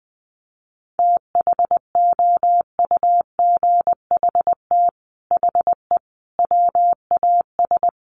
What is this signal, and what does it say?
Morse code, heard as a keyed tone, about 20 wpm, 700 Hz. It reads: THOUGHT HE WAS
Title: Sets of 5 Words Encore2 20wpm